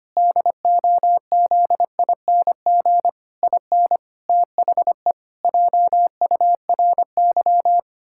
Morse code, heard as a keyed tone, about 25 wpm, 700 Hz. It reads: DOZING IN THE JURY
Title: Pangram Challenge v2 - 25wpm